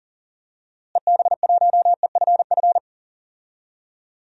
Morse code, heard as a keyed tone, about 40 wpm, 700 Hz. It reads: EB1EFF
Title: DXCC-100 Call Signs - 6 Characters - Repeated 3x - 40wpm